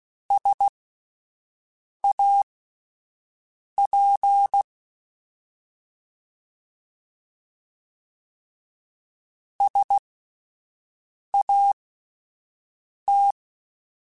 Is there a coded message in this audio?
Yes: Morse code from 0.3 to 13.3 s, heard as a keyed tone, about 16 wpm, spaced as for 5 wpm, 800 Hz. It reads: SAP SAT